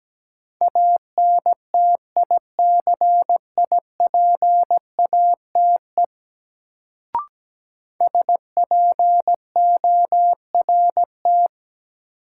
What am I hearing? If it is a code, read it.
Morse code, heard as a keyed tone, about 17 wpm, 700 Hz. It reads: ANTICIPATE SPORT
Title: Sentences from Top 1500 Words - New Word Drill Alt - 17wpm